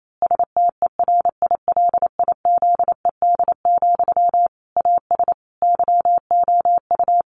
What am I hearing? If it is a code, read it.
Morse code, heard as a keyed tone, about 28 wpm, 700 Hz. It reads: STERILIZED, AS YOU
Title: A Sound Of Thunder 28wpm